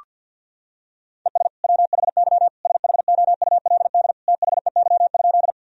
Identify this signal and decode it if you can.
Morse code, heard as a keyed tone, about 46 wpm, 700 Hz. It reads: ES WHY SHOULD THEY?
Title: Sentences from Top 300 Words and Abbreviations 50wpm